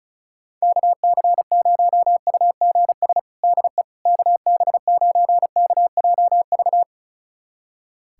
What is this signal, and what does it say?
Morse code, heard as a keyed tone, about 36 wpm, 700 Hz. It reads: KC0UGS DE KB9KJV
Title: USA Callsign DE Callsign - 6 Characters - 35wpm